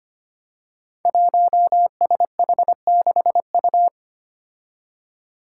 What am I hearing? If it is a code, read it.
Morse code, heard as a keyed tone, about 25 wpm, 700 Hz. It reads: 1SH6U
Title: Five Challenging Letter-Number Combinations 25wpm